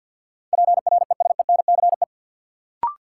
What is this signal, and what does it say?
Morse code, heard as a keyed tone, about 52 wpm, 700 Hz. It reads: PRESENCE
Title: Sets of 3 Words - New Word Drill - 50wpm